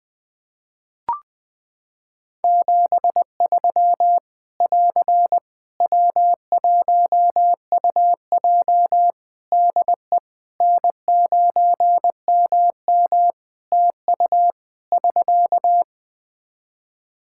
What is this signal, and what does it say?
Morse code, heard as a keyed tone, about 21 wpm, 700 Hz. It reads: 73 <AR> W1UJ DE N9MM TU <SK>
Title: Practice QSO - Exchange 4 - Ending - 20wpm